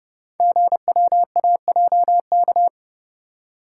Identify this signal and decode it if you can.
Morse code, heard as a keyed tone, about 29 wpm, 700 Hz. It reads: GWAJK
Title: Code Groups - 5 Letter:Numbers - 30wpm